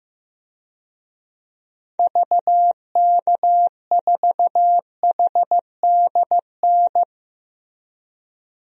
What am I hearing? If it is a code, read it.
Morse code, heard as a keyed tone, about 15 wpm, 700 Hz. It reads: VK4HDN
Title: DXCC-100 Call Signs - 6 Characters - Repeated 3x - 15wpm